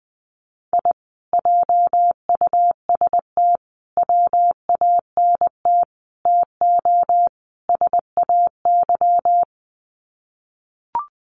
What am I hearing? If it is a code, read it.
Morse code, heard as a keyed tone, about 20 wpm, 700 Hz. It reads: I JUST WANT TO SAY
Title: Sets of 5 Words Encore 20wpm